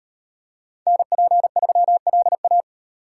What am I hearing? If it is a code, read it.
Morse code, heard as a keyed tone, about 36 wpm, 700 Hz. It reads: NP3LA